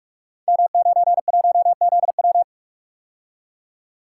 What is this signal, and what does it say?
Morse code, heard as a keyed tone, about 44 wpm, 700 Hz. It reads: M91ZW
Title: Code Groups - 5 Letter:Numbers - 45wpm